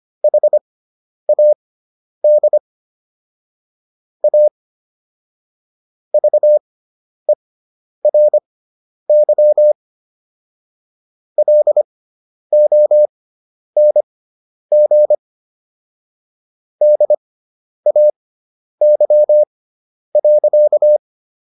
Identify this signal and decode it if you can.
Morse code, heard as a keyed tone, about 25 wpm, spaced as for 10 wpm, 600 Hz. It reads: HAD A VERY LONG DAY.